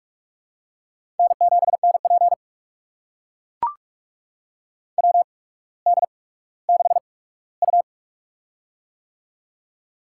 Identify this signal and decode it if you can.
Morse code, heard as a keyed tone, about 43 wpm, spaced as for 19 wpm, 700 Hz. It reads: N7NP WD6U
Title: USA Call Signs - 4 Characters - 8x Character Spacing - 45wpm